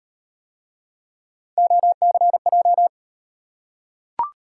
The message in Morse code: OCJ

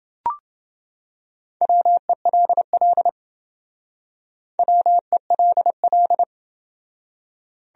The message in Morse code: WELL WELL

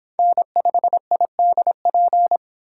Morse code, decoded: N5IDP

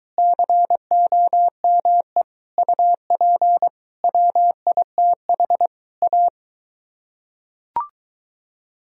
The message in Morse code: COME UP WITH A

